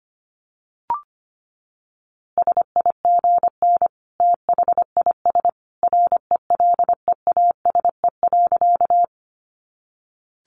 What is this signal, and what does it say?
Morse code, heard as a keyed tone, about 25 wpm, 700 Hz. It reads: SIGN THIS RELEASE.
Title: A Sound Of Thunder 25wpm